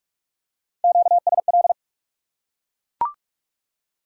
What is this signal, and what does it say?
Morse code, heard as a keyed tone, about 44 wpm, 700 Hz. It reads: QSL
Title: Top 180 QSO Elements 45wpm